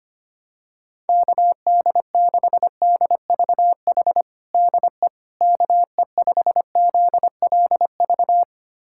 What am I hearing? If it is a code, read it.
Morse code, heard as a keyed tone, about 25 wpm, 700 Hz. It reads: KD6DVH DE KE5ZLV
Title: USA Callsign DE Callsign 25wpm